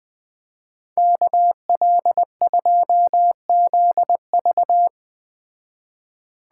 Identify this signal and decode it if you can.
Morse code, heard as a keyed tone, about 20 wpm, 700 Hz. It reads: KL2ZV